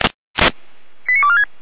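Five-tone selective call (ZVEI), heard as quick runs of five tones, tones of about 70 ms. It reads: 89246
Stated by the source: ZVEI five-tone call 89164 received by SDR, Feuerwehr Orlamünde.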